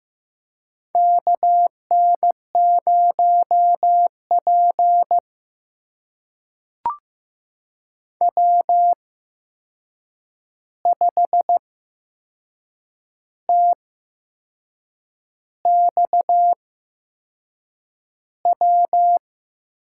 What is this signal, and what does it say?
Morse code, heard as a keyed tone, about 15 wpm, spaced as for 6 wpm, 700 Hz. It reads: KN0P W5TXW
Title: USA Call Signs - 8x Character Spacing - 15wpm